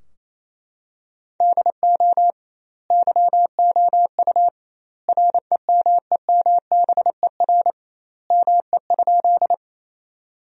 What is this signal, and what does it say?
Morse code, heard as a keyed tone, about 28 wpm, 700 Hz. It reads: DO YOU REMEMBER ME?
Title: 2x Word-Spacing - Sentences from Top 400 Words - 28wpm